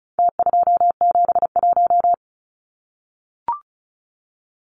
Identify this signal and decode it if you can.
Morse code, heard as a keyed tone, about 34 wpm, 700 Hz. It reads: T271
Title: Numbers - Contest Style - 4-Digit 35wpm